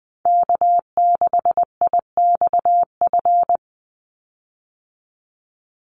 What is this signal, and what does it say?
Morse code, heard as a keyed tone, about 20 wpm, 700 Hz. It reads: K6IXF
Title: USA Call Signs 20wpm